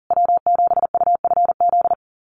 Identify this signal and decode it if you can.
Morse code, heard as a keyed tone, about 40 wpm, 700 Hz. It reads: W7UFZ
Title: USA Call Signs Repeated 3x 40wpm